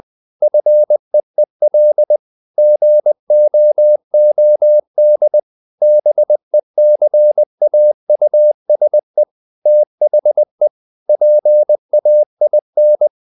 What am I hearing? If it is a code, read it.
Morse code, heard as a keyed tone, about 20 wpm, 600 Hz. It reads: FEEL GOOD BECAUSE THE PAIN
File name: the old man and the sea part 2 20wpm